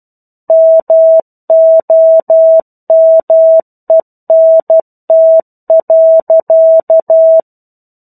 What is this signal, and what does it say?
Morse code, heard as a keyed tone, about 12 wpm, 650 Hz. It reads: MOMENT.
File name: 01-Chapter-01 12 WPM